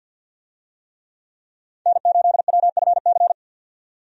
Morse code, heard as a keyed tone, about 50 wpm, 700 Hz. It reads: N8WFC
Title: USA Call Signs - 5 Characters - 50wpm